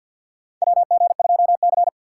Morse code, heard as a keyed tone, about 45 wpm, 700 Hz. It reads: WG2C
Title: USA Call Signs - 4 Characters - 2x Character Spacing - 50wpm